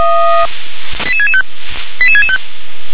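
Five-tone selective call (ZVEI), heard as quick runs of five tones, tones of about 70 ms. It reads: 88665 88665
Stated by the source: ZVEI five-tone selective call received on an SDR.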